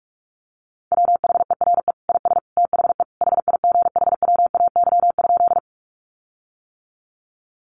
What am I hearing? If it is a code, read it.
Morse code, heard as a keyed tone, about 44 wpm, 700 Hz. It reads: WHERE IS THE HIGHWAY?
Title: Sentences from Top 1400 Words 45wpm